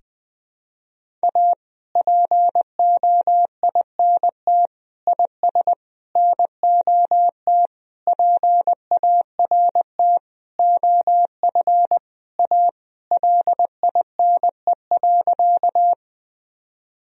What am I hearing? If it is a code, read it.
Morse code, heard as a keyed tone, about 20 wpm, 700 Hz. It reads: A POINT IS NOT PART OF A LINE.
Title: Sentences from Top 1000 Words - Review Part 1 - 20wpm